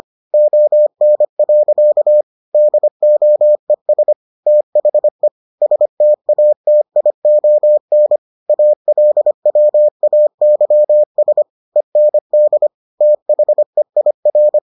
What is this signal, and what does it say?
Morse code, heard as a keyed tone, about 25 wpm, 600 Hz. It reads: ON. DOES THE STATION ALWAYS END THEIR